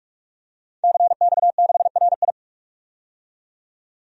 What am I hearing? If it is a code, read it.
Morse code, heard as a keyed tone, about 46 wpm, 700 Hz. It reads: CX6RI
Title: DXCC-100 Call Signs - 5 Characters - 45wpm